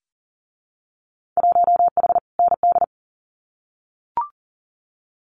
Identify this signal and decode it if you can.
Morse code, heard as a keyed tone, about 42 wpm, 700 Hz. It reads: 1H ND
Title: Winter Field Day Exchanges 40wpm